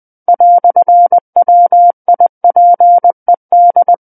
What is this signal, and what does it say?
Morse code, heard as a keyed tone, about 20 wpm, 700 Hz. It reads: "WIPED
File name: 20 WPM